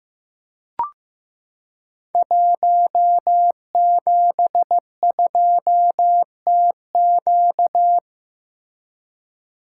Morse code, heard as a keyed tone, about 15 wpm, 700 Hz. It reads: 172TQ
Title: Code Groups - 5 Letter:Numbers - 15wpm